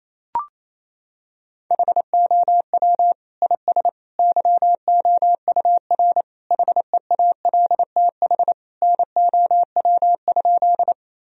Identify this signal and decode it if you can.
Morse code, heard as a keyed tone, about 27 wpm, 700 Hz. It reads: HOW IS YOUR HEALTH NOW?
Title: Sentences from Top 1000 Words - Review Part 3 - 28wpm